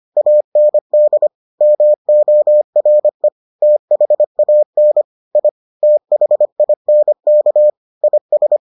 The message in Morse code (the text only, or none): AND MORE THAN I THINK IS